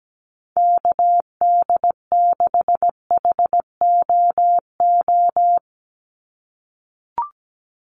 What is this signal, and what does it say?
Morse code, heard as a keyed tone, about 17 wpm, 700 Hz. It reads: KD6HOO
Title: USA Call Signs 17wpm